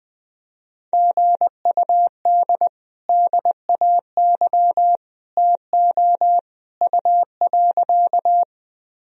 Morse code, heard as a keyed tone, about 21 wpm, 700 Hz. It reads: GUD DAY TO U.
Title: Sentences from Top 200 Words and Abbreviations 20wpm